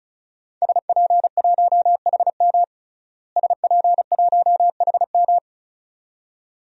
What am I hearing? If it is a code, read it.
Morse code, heard as a keyed tone, about 36 wpm, 700 Hz. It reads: SP1HM SP1HM
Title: DXCC-100 Call Signs - 5 Characters - Repeated 2x - 35wpm